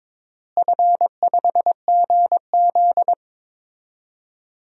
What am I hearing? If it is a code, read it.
Morse code, heard as a keyed tone, about 22 wpm, 700 Hz. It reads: F5GZ
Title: DXCC-100 Call Signs - 4 Characters - Repeated 3x - 22wpm